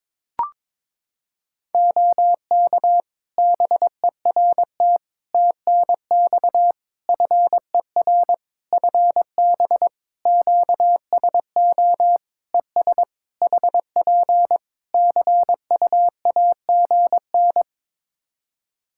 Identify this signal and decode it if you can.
Morse code, heard as a keyed tone, about 22 wpm, 700 Hz. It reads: OK BERT TNX FER FB QSO ES HP CUAGN
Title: Practice QSO - Exchange 4 - Ending - 22wpm